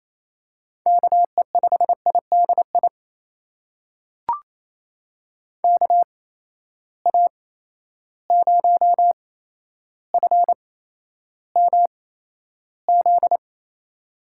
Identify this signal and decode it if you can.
Morse code, heard as a keyed tone, about 28 wpm, spaced as for 12 wpm, 700 Hz. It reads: KE5IDI KA0FMZ
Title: USA Call Signs - 6 Characters - 8x Character Spacing - 28wpm